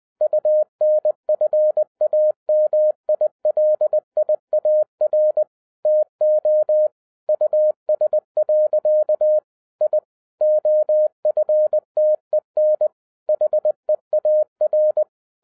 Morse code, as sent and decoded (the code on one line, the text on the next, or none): ..- -. ..-. .- -- .. .-.. .. .- .-. / - --- / ..- ... .-.-.- / .. / --- ..-. - . -. / .... .- .-.
UNFAMILIAR TO US. I OFTEN HEAR